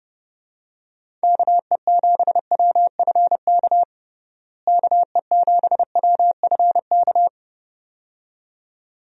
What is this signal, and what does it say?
Morse code, heard as a keyed tone, about 30 wpm, 700 Hz. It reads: KE7WFK KE7WFK
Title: USA Call Signs - 6 Characters - Repeated 2x - 30wpm